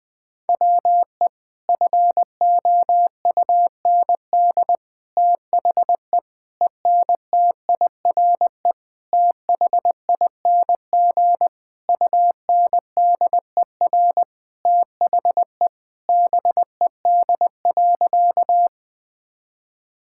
Morse code, heard as a keyed tone, about 20 wpm, 700 Hz. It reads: WE FOUND THE ENTIRE THING UNDER THE BED.
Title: Sentences from Top 1000 Words - Review Part 5 - 20wpm